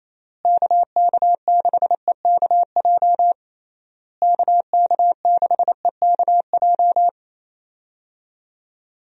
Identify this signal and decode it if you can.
Morse code, heard as a keyed tone, about 28 wpm, 700 Hz. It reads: KK6EKJ KK6EKJ
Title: USA Call Signs - 6 Characters - Repeated 3x - 28wpm